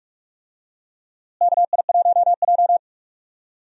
Morse code, heard as a keyed tone, about 46 wpm, 700 Hz. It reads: KI1J
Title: USA Call Signs - 4 Characters - Repeated 2x - 45wpm